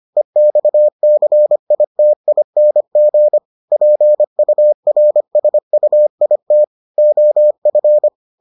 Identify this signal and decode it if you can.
Morse code, heard as a keyed tone, about 25 wpm, 600 Hz. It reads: EXCITING PURSUIT OF